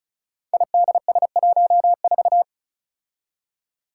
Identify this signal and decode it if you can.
Morse code, heard as a keyed tone, about 36 wpm, 700 Hz. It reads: IDS14
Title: Five Challenging Letter-Number Combinations 35wpm